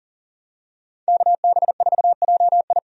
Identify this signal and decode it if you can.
Morse code, heard as a keyed tone, about 40 wpm, 700 Hz. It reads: KB4JI
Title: USA Call Signs - 5 Characters - Repeated 2x - 40wpm